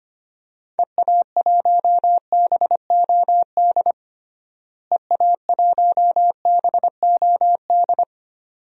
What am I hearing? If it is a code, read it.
Morse code, heard as a keyed tone, about 25 wpm, 700 Hz. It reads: EA1BOD EA1BOD
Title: DXCC-100 Call Signs - 6 Characters - Repeated 2x - 25wpm